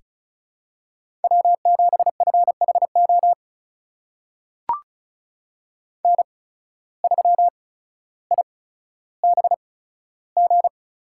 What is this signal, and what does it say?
Morse code, heard as a keyed tone, about 34 wpm, spaced as for 15 wpm, 700 Hz. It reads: W7FHO N3IBG
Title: USA Call Signs - 5 Characters - 8x Character Spacing - 35wpm